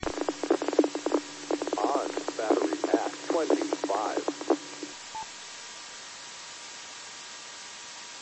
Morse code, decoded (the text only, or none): none